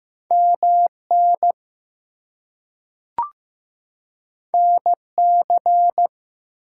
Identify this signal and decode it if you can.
Morse code, heard as a keyed tone, about 15 wpm, 700 Hz. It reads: MN NC
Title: US States and Canadian Provinces 15wpm